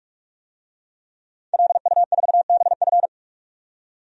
Morse code, heard as a keyed tone, about 43 wpm, 700 Hz. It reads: LU4BF
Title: DXCC-100 Call Signs - 5 Characters - 45wpm